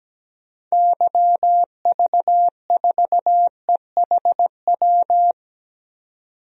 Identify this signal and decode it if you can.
Morse code, heard as a keyed tone, about 17 wpm, 700 Hz. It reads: YV4EHW
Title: DXCC-100 Call Signs - 6 Characters - 17wpm